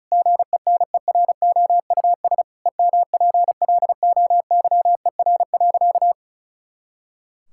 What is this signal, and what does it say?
Morse code, heard as a keyed tone, about 36 wpm, 700 Hz. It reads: GENEROUS EMPLOYER.
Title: Pangram Challenge v2 - 35wpm